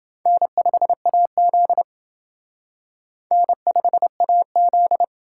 Morse code, heard as a keyed tone, about 28 wpm, 700 Hz. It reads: N5AZ N5AZ